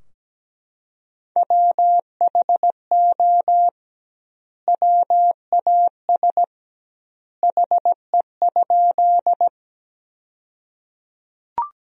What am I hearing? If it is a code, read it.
Morse code, heard as a keyed tone, about 17 wpm, 700 Hz. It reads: WHO WAS HE?
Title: 2x Word-Spacing - Sentences from Top 100 Words - 17wpm